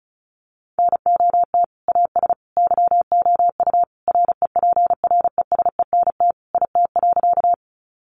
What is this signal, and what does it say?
Morse code, heard as a keyed tone, about 34 wpm, 700 Hz. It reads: NOT AS YOU REPRESENT IT.
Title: Sentences from Top 1200 Words 35wpm